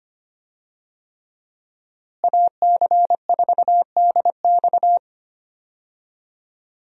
Morse code, heard as a keyed tone, about 25 wpm, 700 Hz. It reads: AC4DX
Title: USA Call Signs - 5 Characters - Repeated 3x - 25wpm